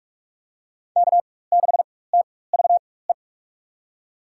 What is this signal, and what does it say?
Morse code, heard as a keyed tone, about 45 wpm, spaced as for 21 wpm, 700 Hz. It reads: K6TVE
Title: USA Call Signs - 5 Characters - 4x Character Spacing - 45wpm